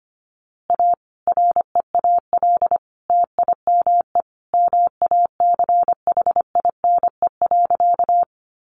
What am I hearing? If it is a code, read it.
Morse code, heard as a keyed tone, about 25 wpm, 700 Hz. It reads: A REAL TIME MACHINE.
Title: A Sound Of Thunder 25wpm